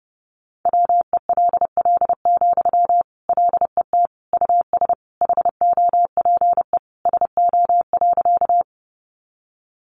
Morse code, heard as a keyed tone, about 30 wpm, 700 Hz. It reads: WELL, LET US HOPE SO.
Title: Sentences from Top 1000 Words 30wpm